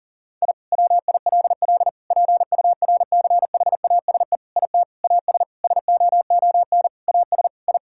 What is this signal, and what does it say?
Morse code, heard as a keyed tone, about 40 wpm, 700 Hz. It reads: I WILL PURCHASE IT AS SOON AS I
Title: Sentences from Top 1000 Words - Review Part 5 - 40wpm